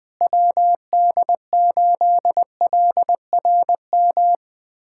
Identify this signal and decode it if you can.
Morse code, heard as a keyed tone, about 20 wpm, 700 Hz. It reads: WD8LRM
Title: USA Callsign DE Callsign - 6 Characters - 20wpm